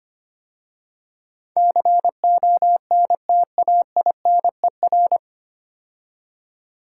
Morse code, heard as a keyed tone, about 25 wpm, 700 Hz. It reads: CONTAINER
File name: practice / 25wpm / 9 Letter Words - First 3 Letters Given - 25wpm